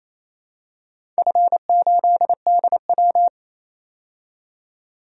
Morse code, heard as a keyed tone, about 28 wpm, 700 Hz. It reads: F8DW
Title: DXCC-100 Call Signs - 4 Characters - 28wpm